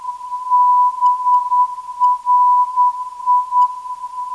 Morse code, heard as a keyed tone, about 10 wpm, 990 Hz. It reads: BRI
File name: BRI - 471.01kHz